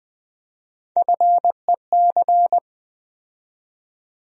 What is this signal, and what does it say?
Morse code, heard as a keyed tone, about 20 wpm, 700 Hz. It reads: FEC
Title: USA Call Sign Postfixes - 3 Letters 20wpm